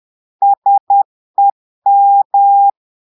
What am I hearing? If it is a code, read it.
Morse code, heard as a keyed tone, about 10 wpm, 800 Hz. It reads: SEM
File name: Portugais - texte en clair 1 - vitesse - 10 mots minute